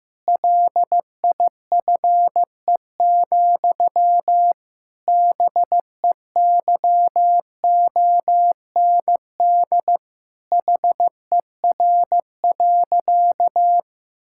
Morse code, heard as a keyed tone, about 15 wpm, 700 Hz. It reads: LIFE, BEYOND HER.
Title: Sentences from Top 1000 Words - Review Part 6 - 15wpm